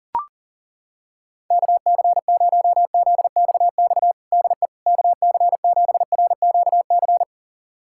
Morse code, heard as a keyed tone, about 40 wpm, 700 Hz. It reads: KC0ZXX DE KC7RQC
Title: USA Callsign DE Callsign 40wpm